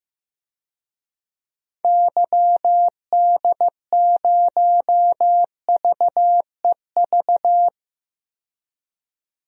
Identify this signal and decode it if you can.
Morse code, heard as a keyed tone, about 15 wpm, 700 Hz. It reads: YD0VEV
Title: DXCC-100 Call Signs - 6 Characters - 15wpm